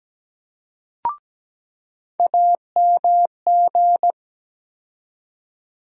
Morse code, heard as a keyed tone, about 17 wpm, 700 Hz. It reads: AMG
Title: USA Call Sign Postfixes - 3 Letters 17wpm